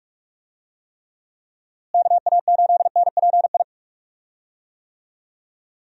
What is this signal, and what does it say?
Morse code, heard as a keyed tone, about 47 wpm, 700 Hz. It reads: KA8NPI
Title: USA Call Signs - 6 Characters - Repeated 2x - 45wpm